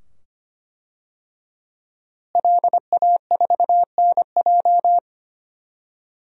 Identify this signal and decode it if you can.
Morse code, heard as a keyed tone, about 25 wpm, 700 Hz. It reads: LA4NJ
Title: DXCC-100 Call Signs - 5 Characters - Repeated 2x - 25wpm